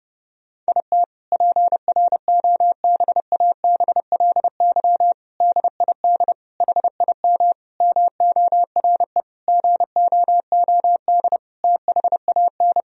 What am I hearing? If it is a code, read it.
Morse code, heard as a keyed tone, about 30 wpm, 700 Hz. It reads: IT PROBABLY DID HIM MORE GOOD THAN
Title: Sentences from Top 1000 Words - Review Part 4 - 30wpm